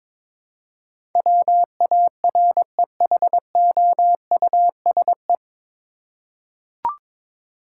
Morse code, heard as a keyed tone, about 22 wpm, 700 Hz. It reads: WAREHOUSE
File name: 9 Letter Words - First 3 Letters Given - 22wpm